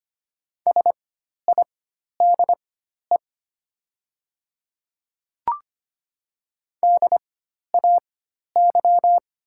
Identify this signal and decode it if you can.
Morse code, heard as a keyed tone, about 25 wpm, spaced as for 12 wpm, 700 Hz. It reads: SIDE DAY
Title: 4x Character Spacing - Top 100 Words - 25wpm